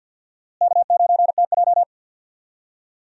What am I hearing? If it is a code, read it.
Morse code, heard as a keyed tone, about 51 wpm, 700 Hz. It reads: K9TJ
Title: USA Call Signs - 4 Characters - Repeated 2x - 50wpm